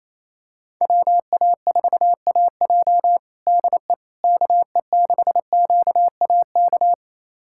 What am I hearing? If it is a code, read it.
Morse code, heard as a keyed tone, about 27 wpm, 700 Hz. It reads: WA4AJ DE KE6QAK
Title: USA Callsign DE Callsign 28wpm